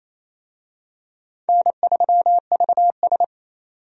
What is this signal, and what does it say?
Morse code, heard as a keyed tone, about 28 wpm, 700 Hz. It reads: N3VS
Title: USA Call Signs - 4 Characters - 28wpm